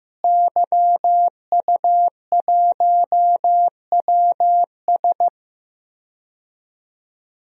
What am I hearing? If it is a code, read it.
Morse code, heard as a keyed tone, about 15 wpm, 700 Hz. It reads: YU1WS
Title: DXCC-100 Call Signs - 5 Characters - Repeated 2x - 15wpm